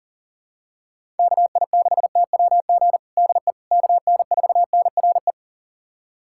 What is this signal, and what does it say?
Morse code, heard as a keyed tone, about 40 wpm, 700 Hz. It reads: KI6TWG DE KN4NRE
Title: USA Callsign DE Callsign - 6 Characters - 40wpm